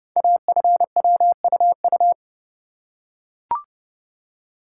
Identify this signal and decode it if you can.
Morse code, heard as a keyed tone, about 30 wpm, 700 Hz. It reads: AFWUU
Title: Code Groups - 5 Letters - 30wpm